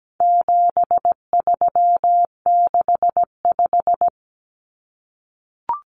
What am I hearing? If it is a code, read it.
Morse code, heard as a keyed tone, about 17 wpm, 700 Hz. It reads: 7365
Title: Common Cut Numbers - 4-Digit 17wpm